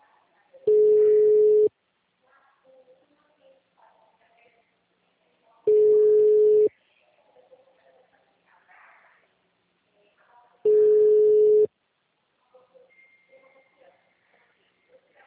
Ringback tone (Europe, 425 Hz).
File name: out- -2002- - - .81411